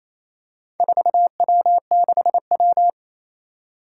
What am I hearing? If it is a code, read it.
Morse code, heard as a keyed tone, about 28 wpm, 700 Hz. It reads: 4W6W
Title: Four Challenging Letter-Number Combinations 28wpm